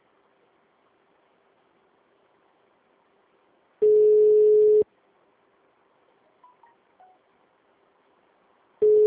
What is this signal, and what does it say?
Ringback tone (Europe, 425 Hz).